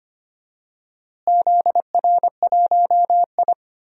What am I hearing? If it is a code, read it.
Morse code, heard as a keyed tone, about 25 wpm, 700 Hz. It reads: ZR1I
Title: DXCC-100 Call Signs - 4 Characters - Repeated 3x - 25wpm